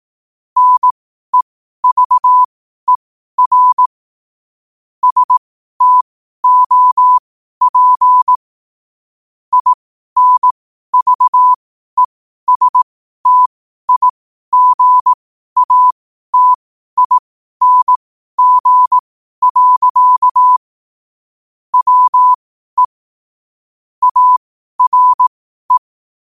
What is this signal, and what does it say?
Morse code, heard as a keyed tone, about 18 wpm, spaced as for 12 wpm, 1000 Hz. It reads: NEVER STOP INVESTIGATING. WE ARE
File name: qotdfiles / 12wpm